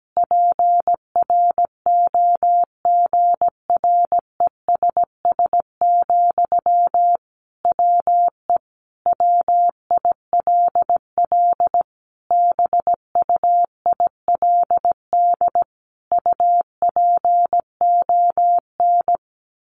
Morse code, heard as a keyed tone, about 17 wpm, 700 Hz. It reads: PROGRESS, WE WILL BUILD UPON
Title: Sentences from Top 1400 Words 17wpm